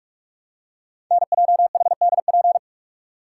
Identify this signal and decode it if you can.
Morse code, heard as a keyed tone, about 44 wpm, 700 Hz. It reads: NJHDP
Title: Code Groups - 5 Letters - 45wpm